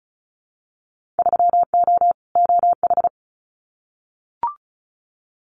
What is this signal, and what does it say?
Morse code, heard as a keyed tone, about 34 wpm, 700 Hz. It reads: 3O OH
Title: Winter Field Day Exchanges 35wpm